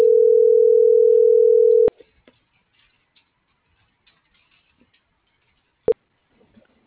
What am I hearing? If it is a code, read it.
Ringback tone (North America), 440 + 480 Hz.